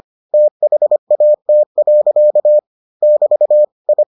Morse code, heard as a keyed tone, about 25 wpm, 600 Hz. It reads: THAT. <BT> I